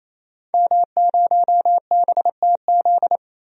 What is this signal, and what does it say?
Morse code, heard as a keyed tone, about 28 wpm, 700 Hz. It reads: M0BTZ
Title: DXCC-100 Call Signs - 5 Characters - Repeated 3x - 28wpm